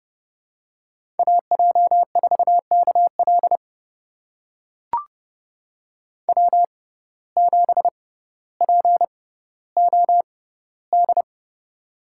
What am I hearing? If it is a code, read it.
Morse code, heard as a keyed tone, about 30 wpm, spaced as for 15 wpm, 700 Hz. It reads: AJ4KL W7POD
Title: USA Call Signs - 5 Characters - 6x Character Spacing - 30wpm